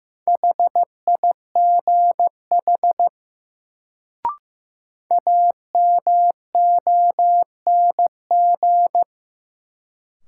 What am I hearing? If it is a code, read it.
Morse code, heard as a keyed tone, about 16 wpm, 700 Hz. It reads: HIGH AMONG
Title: Sets of 2 Words - New Word Drill - 15wpm